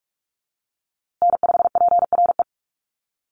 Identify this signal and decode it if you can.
Morse code, heard as a keyed tone, about 44 wpm, 700 Hz. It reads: N5PRE
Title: USA Call Signs Repeated 2x 45wpm